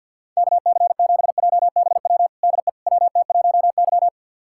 Morse code, heard as a keyed tone, about 47 wpm, 700 Hz. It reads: KC7JBW DE WT1Y
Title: USA Callsign DE Callsign 50wpm